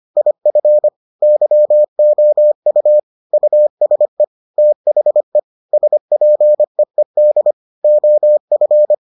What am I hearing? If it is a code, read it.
Morse code, heard as a keyed tone, about 25 wpm, 600 Hz. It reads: IF YOU USE THE SPEED OF